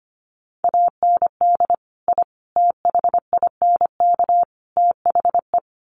Morse code, heard as a keyed tone, about 25 wpm, 700 Hz. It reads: AND I THINK THE
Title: Sets of 4 Words 25wpm